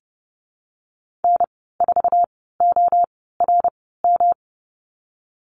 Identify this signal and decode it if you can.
Morse code, heard as a keyed tone, about 31 wpm, spaced as for 17 wpm, 700 Hz. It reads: N4ORM
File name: Lesson 19g - Letter w - Call Signs 30wpm